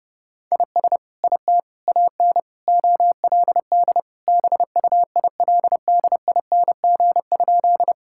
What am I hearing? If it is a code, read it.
Morse code, heard as a keyed tone, about 30 wpm, 700 Hz. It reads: IS IT AN OLD BUILDING?
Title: Sentences from Top 1000 Words - Review Part 5 - 30wpm